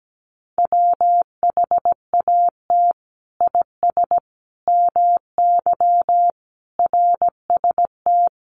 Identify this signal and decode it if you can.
Morse code, heard as a keyed tone, about 17 wpm, 700 Hz. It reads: WHAT IS MY RST
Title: Common QSO Phrases 17wpm